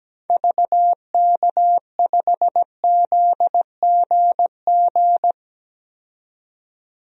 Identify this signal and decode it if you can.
Morse code, heard as a keyed tone, about 17 wpm, 700 Hz. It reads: VK5ZGG
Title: DXCC-100 Call Signs - 6 Characters - 17wpm